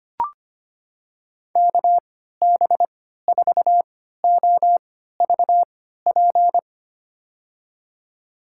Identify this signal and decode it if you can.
Morse code, heard as a keyed tone, about 25 wpm, spaced as for 14 wpm, 690 Hz. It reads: KB4OVP
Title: USA Call Signs - 6 Characters - 3x Character Spacing - 25wpm